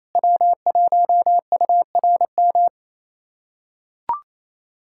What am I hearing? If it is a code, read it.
Morse code, heard as a keyed tone, about 28 wpm, 700 Hz. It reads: W1URM
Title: USA Callsign DE Callsign - 5 Characters - 28wpm